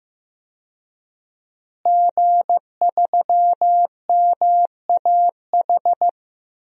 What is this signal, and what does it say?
Morse code, heard as a keyed tone, about 15 wpm, 700 Hz. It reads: G3MAH
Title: DXCC-100 Call Signs - 5 Characters - Repeated 2x - 15wpm